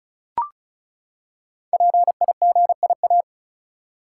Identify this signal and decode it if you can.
Morse code, heard as a keyed tone, about 36 wpm, 700 Hz. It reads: PIGIA
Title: Code Groups - 5 Letters - 35wpm